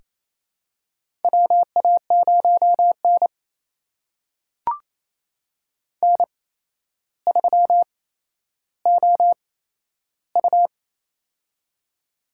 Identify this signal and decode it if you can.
Morse code, heard as a keyed tone, about 28 wpm, spaced as for 12 wpm, 700 Hz. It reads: WA0N N3OU
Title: USA Call Signs - 4 Characters - 8x Character Spacing - 28wpm